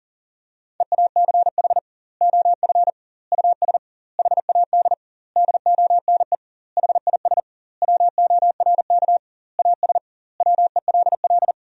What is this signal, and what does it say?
Morse code, heard as a keyed tone, about 40 wpm, 700 Hz. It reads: EACH OF US HAD DONE HIS WORK AS WELL
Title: 2x Word-Spacing - Sentences from Top 500 Words - 40wpm